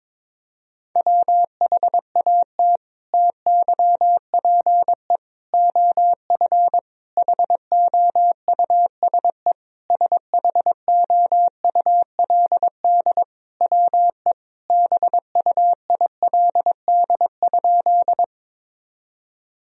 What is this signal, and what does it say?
Morse code, heard as a keyed tone, about 22 wpm, 700 Hz. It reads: WHAT TYPE OF HOUSE SHOULD WE BUILD?
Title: Sentences from Top 1000 Words - Review Part 3 - 22wpm